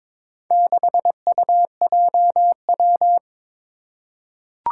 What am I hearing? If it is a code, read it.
Morse code, heard as a keyed tone, about 22 wpm, 700 Hz. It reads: 6UJW
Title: Four Challenging Letter-Number Combinations 22wpm